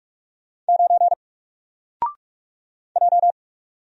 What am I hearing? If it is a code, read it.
Morse code, heard as a keyed tone, about 44 wpm, 700 Hz. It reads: 9 J
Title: Single Character (Letter:Number:Punctuation:Prosign) 45wpm